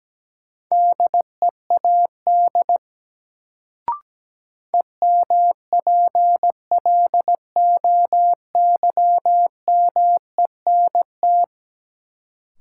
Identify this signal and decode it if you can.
Morse code, heard as a keyed tone, about 17 wpm, 700 Hz. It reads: DEAD EMPLOYMENT
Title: Sentences from Top 1500 Words - New Word Drill - 17wpm